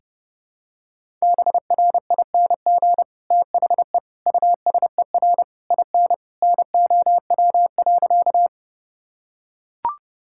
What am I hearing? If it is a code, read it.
Morse code, heard as a keyed tone, about 29 wpm, 700 Hz. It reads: BRING THE USER IN NOW.
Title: Sentences from Top 1000 Words - Review Part 5 - 30wpm